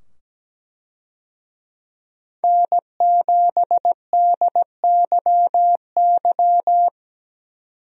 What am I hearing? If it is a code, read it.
Morse code, heard as a keyed tone, about 17 wpm, 700 Hz. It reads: N7DYY
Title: USA Call Signs - 5 Characters - 17wpm